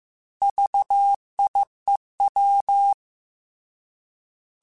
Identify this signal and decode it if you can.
Morse code, heard as a keyed tone, about 15 wpm, 800 Hz. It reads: VIEW
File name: Alphabetical / Alphabetical mp3s / Vs @13